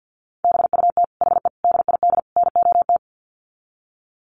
Morse code, heard as a keyed tone, about 50 wpm, 700 Hz. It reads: BUT HE DID NOT
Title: Sets of 4 Words 50wpm